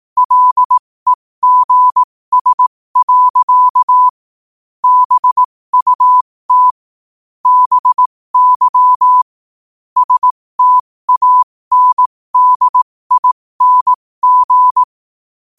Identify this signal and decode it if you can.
Morse code, heard as a keyed tone, about 18 wpm, spaced as for 15 wpm, 1000 Hz. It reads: LEGS. BUT BY STANDING
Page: 15 wpm morse code quotes